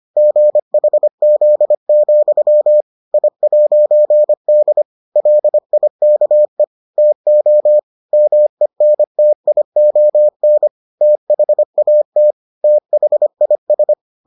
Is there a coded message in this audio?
Yes: Morse code from 0.2 to 13.9 s, heard as a keyed tone, about 25 wpm, 600 Hz. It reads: GHZ, I'D LIKE TO MENTION THAT THIS